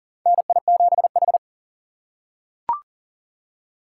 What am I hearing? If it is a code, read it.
Morse code, heard as a keyed tone, about 40 wpm, 700 Hz. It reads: NI7H
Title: USA Call Signs - 3x Character Spacing - 40wpm